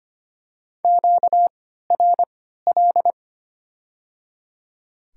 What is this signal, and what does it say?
Morse code, heard as a keyed tone, about 25 wpm, spaced as for 14 wpm, 700 Hz. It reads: QRL
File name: Top 20 QSO Elements - 3x Character Spacing - 25wpm